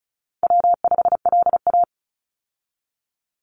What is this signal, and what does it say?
Morse code, heard as a keyed tone, about 36 wpm, 700 Hz. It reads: W5LA
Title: Call Signs - 35wpm